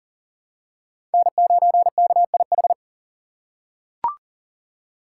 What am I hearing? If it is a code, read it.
Morse code, heard as a keyed tone, about 40 wpm, 700 Hz. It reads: N9KIH
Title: USA Call Signs - 5 Characters - 4x Character Spacing - 40wpm